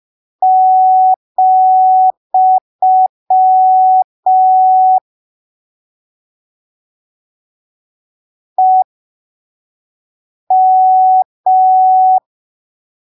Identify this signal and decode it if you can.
Morse code, heard as a keyed tone, about 5 wpm, spaced as for 3 wpm, 750 Hz. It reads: , EM
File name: arquivos / texto / texto 08 05ppm